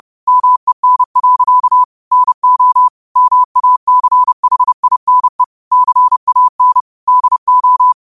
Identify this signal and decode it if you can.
Morse code, heard as a keyed tone, about 30 wpm, 1000 Hz. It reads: MEN. NO MACHINE CAN DO